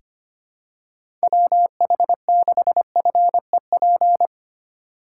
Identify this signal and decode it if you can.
Morse code, heard as a keyed tone, about 25 wpm, 700 Hz. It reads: WH6FEP
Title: USA Call Signs - 6 Characters - 4x Character Spacing - 25wpm